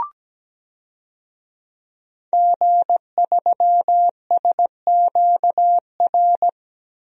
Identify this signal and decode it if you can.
Morse code, heard as a keyed tone, about 17 wpm, 700 Hz. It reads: G3SQR
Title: DXCC-100 Call Signs - 5 Characters - Repeated 2x - 17wpm